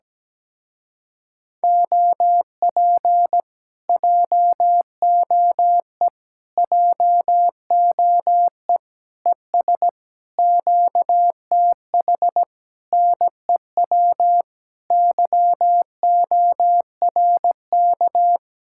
Morse code, heard as a keyed tone, about 17 wpm, 700 Hz. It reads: OP JOE JOE ES QTH NEW YORK
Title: Speed-Racing Common QSO Phrases - 17wpm